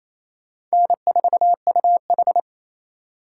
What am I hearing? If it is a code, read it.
Morse code, heard as a keyed tone, about 36 wpm, 700 Hz. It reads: N4UH